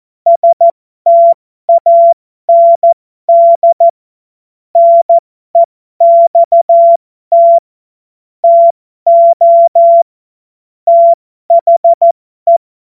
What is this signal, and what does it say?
Morse code, heard as a keyed tone, about 14 wpm, spaced as for 12 wpm, 680 Hz. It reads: STAND NEXT TO THE
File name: Morsekurs / Witze